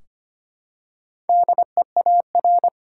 Morse code, heard as a keyed tone, about 25 wpm, 700 Hz. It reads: DEAR
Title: Four Letter Words - 25wpm